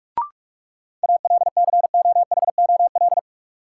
Morse code, heard as a keyed tone, about 47 wpm, 700 Hz. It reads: ALCOHOL
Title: Sentences from Top 1400 Words - New Word Drill Alt - 45wpm